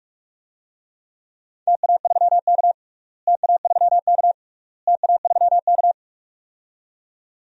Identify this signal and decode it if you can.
Morse code, heard as a keyed tone, about 44 wpm, 700 Hz. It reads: TA3K TA3K TA3K
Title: DXCC-100 Call Signs - 4 Characters - Repeated 3x - 45wpm